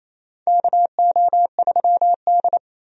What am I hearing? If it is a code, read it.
Morse code, heard as a keyed tone, about 28 wpm, 700 Hz. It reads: KO3D
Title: USA Call Signs - 4 Characters - 28wpm